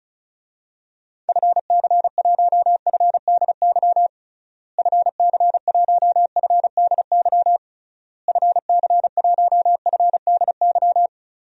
Morse code, heard as a keyed tone, about 34 wpm, 700 Hz. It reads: FC1FDY FC1FDY FC1FDY
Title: DXCC-100 Call Signs - 6 Characters - Repeated 3x - 35wpm